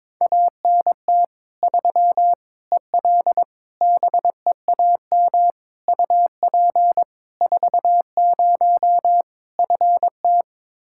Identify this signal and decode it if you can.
Morse code, heard as a keyed tone, about 22 wpm, 700 Hz. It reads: ANT 3 EL BEAM UP 40 FT
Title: Practice QSO - Exchange 2 - Info - 22wpm